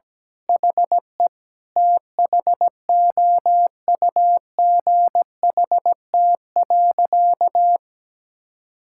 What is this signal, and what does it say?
Morse code, heard as a keyed tone, about 17 wpm, 700 Hz. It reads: HE THOUGHT.
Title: Sentences from Top 1000 Words - Review Part 6 - 17wpm